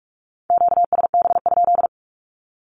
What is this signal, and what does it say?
Morse code, heard as a keyed tone, about 46 wpm, 700 Hz. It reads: QSB?
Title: Q-Codes with Spoken Meaning - 45wpm